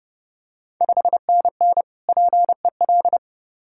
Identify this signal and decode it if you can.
Morse code, heard as a keyed tone, about 30 wpm, 700 Hz. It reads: 5NN PEL
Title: Ontario QSO Party Exchange 30wpm